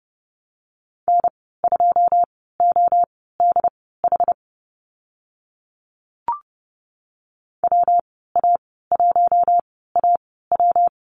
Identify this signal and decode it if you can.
Morse code, heard as a keyed tone, about 30 wpm, spaced as for 17 wpm, 700 Hz. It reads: N2ODH WA1AW
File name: Lesson 19g - Letter w - Call Signs 30wpm